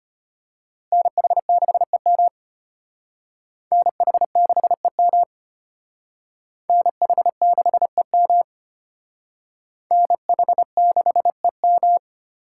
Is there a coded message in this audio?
Yes: Morse code from 0.9 to 12.0 s, heard as a keyed tone, about 32 wpm, 700 Hz. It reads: NH6EM NH6EM NH6EM NH6EM